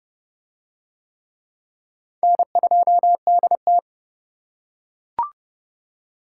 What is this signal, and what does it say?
Morse code, heard as a keyed tone, about 26 wpm, 700 Hz. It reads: N2DT